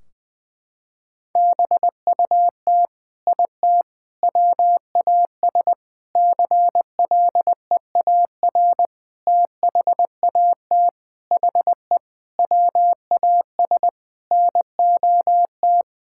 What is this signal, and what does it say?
Morse code, heard as a keyed tone, about 20 wpm, 700 Hz. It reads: BUT IT WAS CLEAR THAT HE WAS NOT
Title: Sentences from Top 1600 Words 20wpm